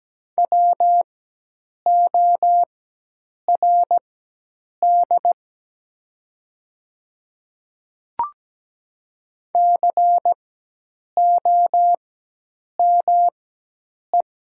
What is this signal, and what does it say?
Morse code, heard as a keyed tone, about 17 wpm, spaced as for 8 wpm, 700 Hz. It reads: WORD COME
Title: 4x Character Spacing - Top 100 Words - 17wpm